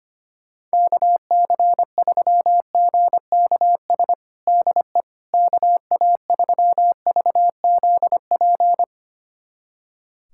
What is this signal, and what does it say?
Morse code, heard as a keyed tone, about 25 wpm, 700 Hz. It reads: KC3GKS DE KA3VZP
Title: USA Callsign DE Callsign - 6 Characters - 25wpm